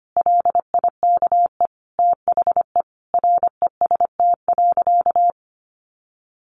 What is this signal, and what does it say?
Morse code, heard as a keyed tone, about 25 wpm, 700 Hz. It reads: LIKE THE REST.
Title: Sentences from Top 1100 Words 25wpm